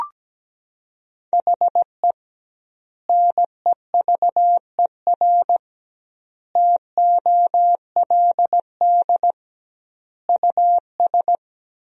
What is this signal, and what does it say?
Morse code, heard as a keyed tone, about 17 wpm, 700 Hz. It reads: HE NEVER TOLD US
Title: 2x Word-Spacing - Sentences from Top 400 Words - 17wpm